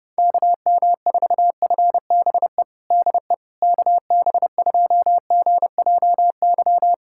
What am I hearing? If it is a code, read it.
Morse code, heard as a keyed tone, about 30 wpm, 700 Hz. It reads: KM4FBE DE KB2GJY
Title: USA Callsign DE Callsign - 6 Characters - 30wpm